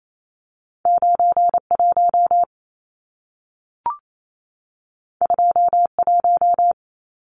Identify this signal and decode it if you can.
Morse code, heard as a keyed tone, about 28 wpm, 700 Hz. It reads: 91 21
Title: Numbers - 2-Digit 28wpm